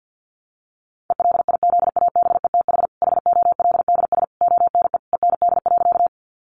Morse code, heard as a keyed tone, about 50 wpm, 700 Hz. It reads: ELIZABETH HOLDS ONE END.
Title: Easy Phrases 50wpm